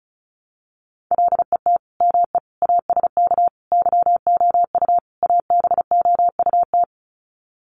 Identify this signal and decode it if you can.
Morse code, heard as a keyed tone, about 34 wpm, 700 Hz. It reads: LET ME ASK YOU ABOUT
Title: Sets of 5 Words 35wpm